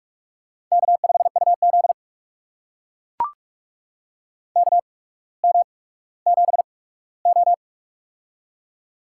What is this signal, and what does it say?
Morse code, heard as a keyed tone, about 44 wpm, spaced as for 19 wpm, 700 Hz. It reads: K5UZ KM7O
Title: USA Call Signs - 4 Characters - 8x Character Spacing - 45wpm